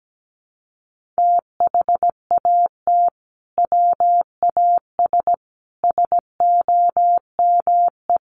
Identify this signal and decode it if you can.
Morse code, heard as a keyed tone, about 17 wpm, 700 Hz. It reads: THAT WAS SOME
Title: Sentences from Top 1400 Words 17wpm